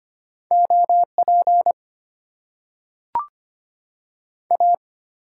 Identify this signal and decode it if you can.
Morse code, heard as a keyed tone, about 25 wpm, 700 Hz. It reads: OP A
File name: Top 90 QSO Elements - 8x Character Spacing - 25wpm